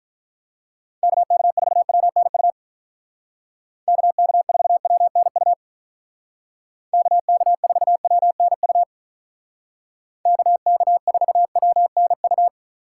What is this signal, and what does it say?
Morse code, heard as a keyed tone, about 43 wpm, 700 Hz. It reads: KK4WNU KK4WNU KK4WNU KK4WNU
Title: Speed Racing - USA 2x3 Call Signs 35wpm